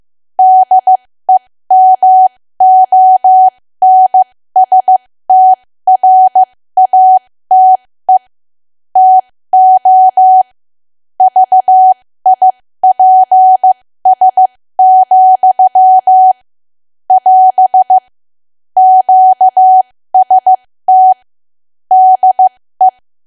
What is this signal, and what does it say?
Morse code, heard as a keyed tone, about 15 wpm, spaced as for 13 wpm, 750 Hz. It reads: DEMONSTRATE TO VIPS, <AS> QST DE
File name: cwa / W1AW-Level2 / 13WPM